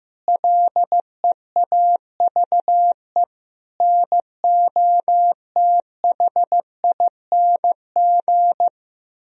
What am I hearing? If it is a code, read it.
Morse code, heard as a keyed tone, about 15 wpm, 700 Hz. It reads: LEAVE NOTHING
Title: Sentences from Top 1000 Words - Review Part 1 - 15wpm